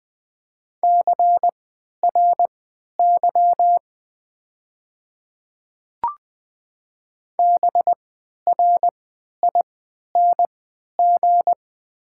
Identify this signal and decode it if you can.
Morse code, heard as a keyed tone, about 20 wpm, spaced as for 11 wpm, 690 Hz. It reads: CRY BRING
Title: 3x Character Spacing - Top 500 Words - 20wpm